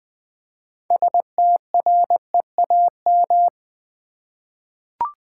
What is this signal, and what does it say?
Morse code, heard as a keyed tone, about 20 wpm, 700 Hz. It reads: STREAM
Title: 6 Letter Words - First 3 Letters Given - 20wpm